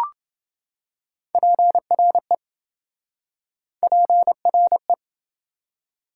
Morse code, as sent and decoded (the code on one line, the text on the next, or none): .--. .-. / .--. .-. .
PRE PRE